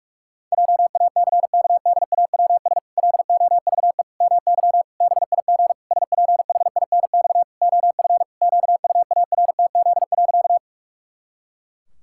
Morse code, heard as a keyed tone, about 46 wpm, 700 Hz. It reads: JACKDAWS LOVE MY BIG SPHINX OF QUARTZ.
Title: Pangram Challenge v2 - 45wpm